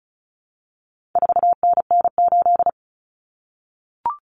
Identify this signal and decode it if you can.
Morse code, heard as a keyed tone, about 34 wpm, 700 Hz. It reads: 4NN8
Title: Common Cut Numbers - 4-Digit 35wpm